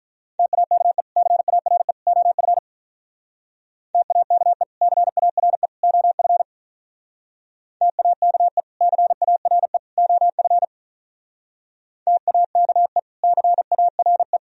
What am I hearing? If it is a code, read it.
Morse code, heard as a keyed tone, about 45 wpm, 700 Hz. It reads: TAKE CARE OF TAKE CARE OF TAKE CARE OF TAKE CARE